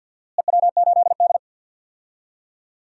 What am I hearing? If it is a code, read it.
Morse code, heard as a keyed tone, about 51 wpm, 700 Hz. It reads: EW8D
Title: DXCC-100 Call Signs - 4 Characters - Repeated 2x - 50wpm